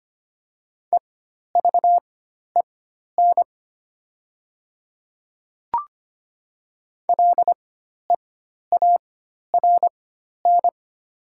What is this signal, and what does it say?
Morse code, heard as a keyed tone, about 25 wpm, spaced as for 12 wpm, 700 Hz. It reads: EVEN LEARN
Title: 4x Character Spacing - Top 300 Words - 25wpm